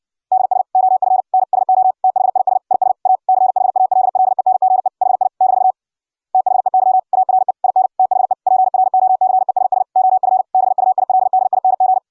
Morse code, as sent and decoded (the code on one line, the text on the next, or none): -.. ... / --- ..-. / -. / ... --- / - .... .. ... / .. ... / .- / -.-. .... .- .-.. .-.. -. --. / .... .. / -....- / - .... -.-- / .- .-. / .. -. / - .... / .--. .-. --- -.-. ... ... / --- ..-. / -... ..- .. .-.. -.. .. -. --.
DS OF N SO THIS IS A CHALLENGE HI - THEY ARE IN THE PROCESS OF BUILDING